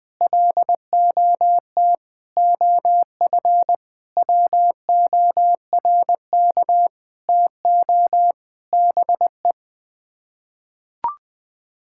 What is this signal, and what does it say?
Morse code, heard as a keyed tone, about 21 wpm, 700 Hz. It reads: LOT OF WORK TO BE
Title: Sets of 9 Words - 20wpm